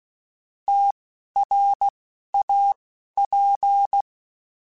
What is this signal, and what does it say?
Morse code, heard as a keyed tone, about 16 wpm, spaced as for 11 wpm, 800 Hz. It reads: TRAP
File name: QSOs / QSOs-mp3s / WD0FBT 1@10